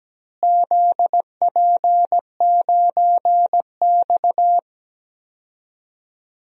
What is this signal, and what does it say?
Morse code, heard as a keyed tone, about 17 wpm, 700 Hz. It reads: ZP9X
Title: DXCC-100 Call Signs - 4 Characters - Repeated 3x - 17wpm